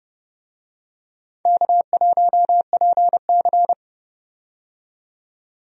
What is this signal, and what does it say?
Morse code, heard as a keyed tone, about 31 wpm, 700 Hz. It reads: K1PC
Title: USA Call Signs - 4 Characters - 30wpm